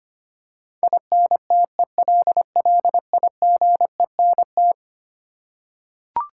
Morse code, heard as a keyed tone, about 25 wpm, 700 Hz. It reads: INTELLIGENT
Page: morse-code-ninja-practice-25wpm